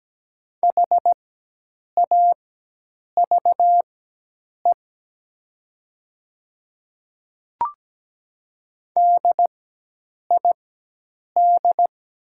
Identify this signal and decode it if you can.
Morse code, heard as a keyed tone, about 17 wpm, spaced as for 8 wpm, 700 Hz. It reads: HAVE DID
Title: 4x Character Spacing - Top 100 Words - 17wpm